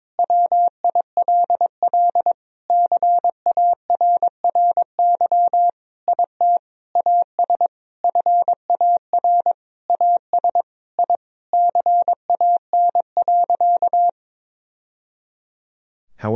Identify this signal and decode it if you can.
Morse code, heard as a keyed tone, about 22 wpm, 700 Hz. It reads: WILL CARRY IT AS FAR AS I CAN.
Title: Sentences from Top 1000 Words - Review Part 1 - 22wpm